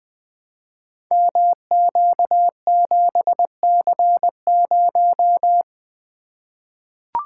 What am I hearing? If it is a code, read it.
Morse code, heard as a keyed tone, about 20 wpm, 700 Hz. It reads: MQ7C0
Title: Code Groups - 5 Letter:Numbers - 20wpm